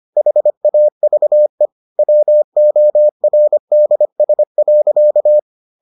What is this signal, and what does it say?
Morse code, heard as a keyed tone, about 25 wpm, 600 Hz. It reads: HAVE WORDS.